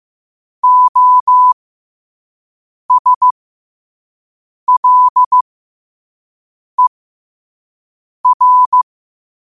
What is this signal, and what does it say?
Morse code, heard as a keyed tone, about 15 wpm, spaced as for 5 wpm, 1000 Hz. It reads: OSLER
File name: qotdfiles / 5wpm